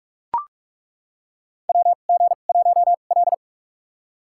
Morse code, heard as a keyed tone, about 43 wpm, spaced as for 32 wpm, 700 Hz. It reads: WG1L
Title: USA Call Signs - 4 Characters - 2x Character Spacing - 45wpm